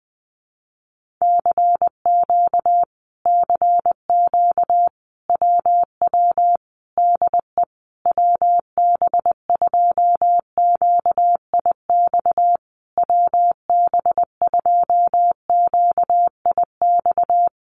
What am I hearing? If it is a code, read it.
Morse code, heard as a keyed tone, about 20 wpm, 700 Hz. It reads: CQ CQ WW DE WB2QIX WB2QIX
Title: Common QSO Phrases Encore 20wpm